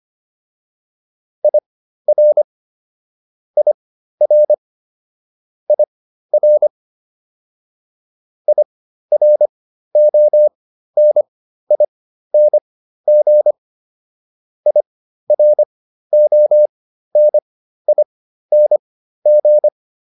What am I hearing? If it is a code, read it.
Morse code, heard as a keyed tone, about 25 wpm, spaced as for 13 wpm, 600 Hz. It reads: IR IR IR IRONING IRONING